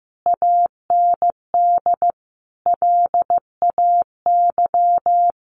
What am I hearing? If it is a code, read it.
Morse code, heard as a keyed tone, about 15 wpm, 700 Hz. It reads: AND LAY
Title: Sentences from Top 1100 Words 15wpm